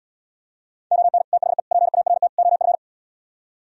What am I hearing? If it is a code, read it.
Morse code, heard as a keyed tone, about 77 wpm, 700 Hz. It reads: ON THE PART OF